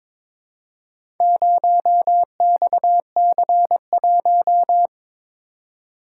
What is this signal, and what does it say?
Morse code, heard as a keyed tone, about 22 wpm, 700 Hz. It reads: 0XC1
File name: practice / 22wpm / Four Letter-Number Combinations - 22wpm